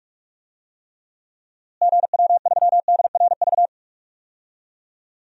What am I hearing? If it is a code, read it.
Morse code, heard as a keyed tone, about 46 wpm, 700 Hz. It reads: GW3DRV
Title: DXCC-100 Call Signs - 6 Characters - Repeated 3x - 45wpm